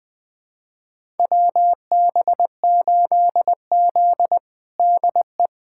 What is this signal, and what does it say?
Morse code, heard as a keyed tone, about 20 wpm, 700 Hz. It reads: WB8Z DE
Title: USA Callsign DE Callsign - 4 Characters - 20wpm